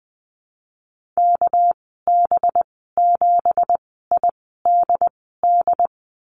Morse code, heard as a keyed tone, about 20 wpm, spaced as for 15 wpm, 700 Hz. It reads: KB7IDD
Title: 213 Lesson 34g - Number 8 - Call Signs 20wpm